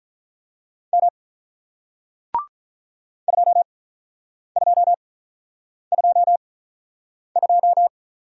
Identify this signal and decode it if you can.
Morse code, heard as a keyed tone, about 48 wpm, spaced as for 9 wpm, 700 Hz. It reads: M 2222